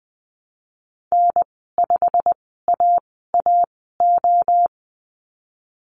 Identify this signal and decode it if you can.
Morse code, heard as a keyed tone, about 20 wpm, spaced as for 15 wpm, 700 Hz. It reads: N5AAO
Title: 088 Lesson 15g - Number 5 - Call Signs 20wpm